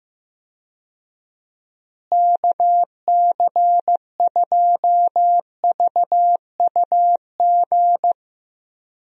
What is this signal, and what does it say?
Morse code, heard as a keyed tone, about 15 wpm, 700 Hz. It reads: KC2VUG
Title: USA Call Signs - 6 Characters - Repeated 3x - 15wpm